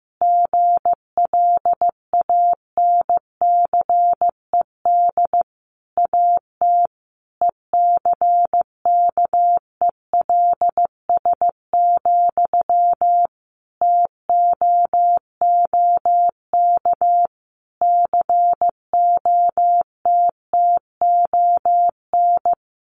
Morse code, heard as a keyed tone, about 15 wpm, 700 Hz. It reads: GLANCED AT ECKELS, TOOK COTTON
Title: A Sound Of Thunder 15wpm